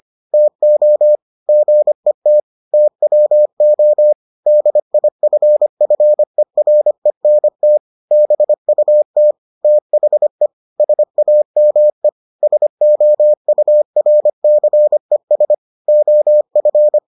Morse code, heard as a keyed tone, about 25 wpm, 600 Hz. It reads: TO GET TWO DIFFERENT BUT THE SAME SOURCES OF